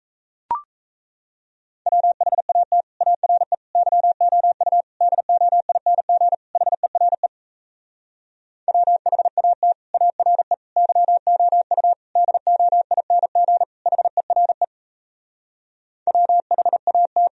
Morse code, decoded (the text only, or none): WHAT ARE YOU DOING HERE WHAT ARE YOU DOING HERE WHAT